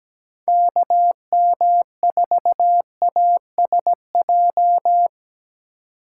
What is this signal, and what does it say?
Morse code, heard as a keyed tone, about 20 wpm, 700 Hz. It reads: KM4ASJ
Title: Speed Racing - USA 2x3 Call Signs 17wpm